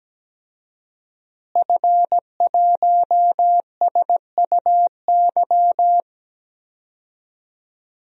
Morse code, heard as a keyed tone, about 17 wpm, 700 Hz. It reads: F1SUY
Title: DXCC-100 Call Signs - 5 Characters - Repeated 2x - 17wpm